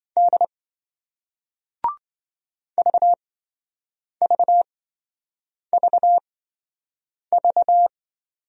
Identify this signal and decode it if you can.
Morse code, heard as a keyed tone, about 27 wpm, spaced as for 7 wpm, 700 Hz. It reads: D VVVV